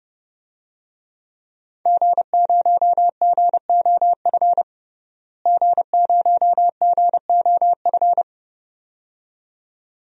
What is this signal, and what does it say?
Morse code, heard as a keyed tone, about 30 wpm, 700 Hz. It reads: G0GOF G0GOF
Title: DXCC-100 Call Signs - 5 Characters - Repeated 2x - 30wpm